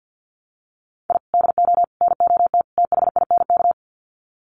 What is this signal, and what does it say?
Morse code, heard as a keyed tone, about 50 wpm, 700 Hz. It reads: I DO NOT THINK
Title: Sets of 4 Words 50wpm